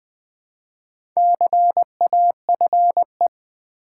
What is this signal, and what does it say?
Morse code, heard as a keyed tone, about 20 wpm, 700 Hz. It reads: CAFE
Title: Build A Word - 2 Letters At A Time - 4-Letter Words - 20wpm